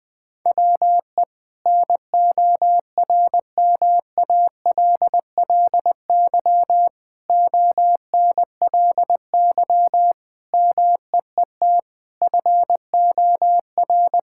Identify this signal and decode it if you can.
Morse code, heard as a keyed tone, about 21 wpm, 700 Hz. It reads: WE NORMALLY ONLY MEET FOR
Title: Sentences from Top 1000 Words - Review Part 6 - 20wpm